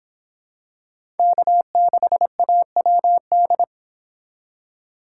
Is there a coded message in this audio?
Yes: Morse code from 1.2 to 3.6 s, heard as a keyed tone, about 26 wpm, 700 Hz. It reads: K6AWD